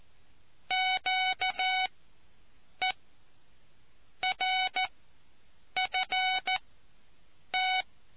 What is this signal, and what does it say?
Morse code, heard as a keyed tone, about 14 wpm, spaced as for 6 wpm, 750 Hz. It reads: QERFT